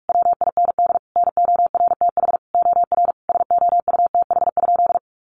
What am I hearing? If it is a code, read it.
Morse code, heard as a keyed tone, about 44 wpm, 700 Hz. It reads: WIND NORTH OR SOUTH?
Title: Sentences from Top 400 Words 45wpm